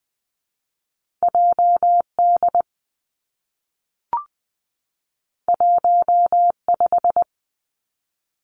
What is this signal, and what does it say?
Morse code, heard as a keyed tone, about 20 wpm, 700 Hz. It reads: J D 1 5
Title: Two Challenging Letter-Number Combinations 20wpm